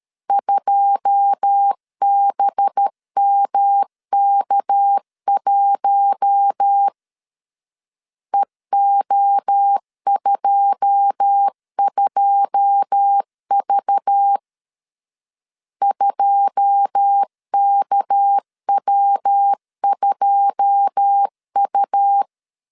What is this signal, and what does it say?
Morse code, heard as a keyed tone, about 13 wpm, 800 Hz. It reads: 2BMK1 EO22V 2KW2U
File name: LEZIONI / 9 2 12wpm 2301